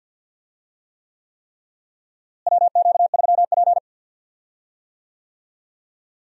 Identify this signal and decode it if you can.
Morse code, heard as a keyed tone, about 51 wpm, 700 Hz. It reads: WQ3P
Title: USA Call Signs - 4 Characters - Repeated 3x - 50wpm